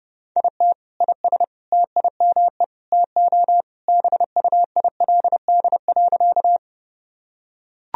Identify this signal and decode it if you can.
Morse code, heard as a keyed tone, about 30 wpm, 700 Hz. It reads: IT IS TIME TO BUILD.
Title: Sentences from Top 1000 Words - Review Part 1 - 30wpm